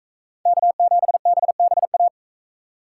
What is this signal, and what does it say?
Morse code, heard as a keyed tone, about 41 wpm, 700 Hz. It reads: K7BBA